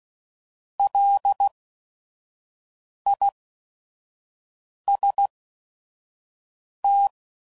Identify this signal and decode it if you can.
Morse code, heard as a keyed tone, about 16 wpm, spaced as for 5 wpm, 800 Hz. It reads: LIST